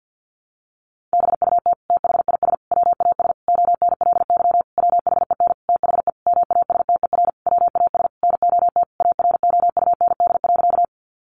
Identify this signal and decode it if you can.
Morse code, heard as a keyed tone, about 50 wpm, 700 Hz. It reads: BUT THIS WAS ONLY WHEN THE MASTER WAS NOT AROUND.
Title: Sentences from Top 1300 Words 50wpm